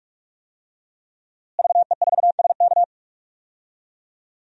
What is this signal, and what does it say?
Morse code, heard as a keyed tone, about 46 wpm, 700 Hz. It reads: VE4SK
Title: DXCC-100 Call Signs - 5 Characters - Repeated 2x - 45wpm